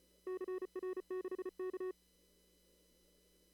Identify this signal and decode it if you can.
Morse code, heard as a keyed tone, about 34 wpm, 370 Hz. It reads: CR6K
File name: CW / kt5j wpx cw-op-k5pi